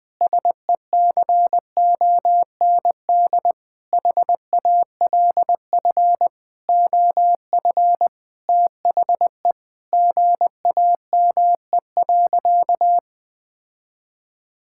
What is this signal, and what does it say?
Morse code, heard as a keyed tone, about 20 wpm, 700 Hz. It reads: SECOND HALF OF THE GAME.
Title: Sentences with 5-Word Phrases - 20wpm